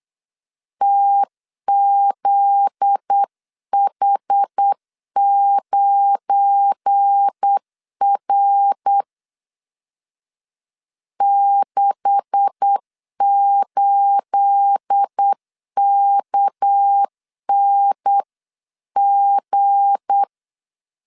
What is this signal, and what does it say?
Morse code, heard as a keyed tone, about 8 wpm, 800 Hz. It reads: TZH9R 68KNG
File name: CW / DET / 1 9 1500C S B 8wpm 5624